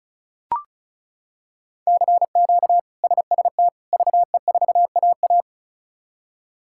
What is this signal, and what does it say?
Morse code, heard as a keyed tone, about 36 wpm, 700 Hz. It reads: CQ SST VE4AA
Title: K1USN SST QSO Practice - 35wpm